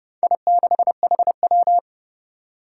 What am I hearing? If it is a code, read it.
Morse code, heard as a keyed tone, about 30 wpm, 700 Hz. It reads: I6HW
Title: Four Challenging Letter-Number Combinations 30wpm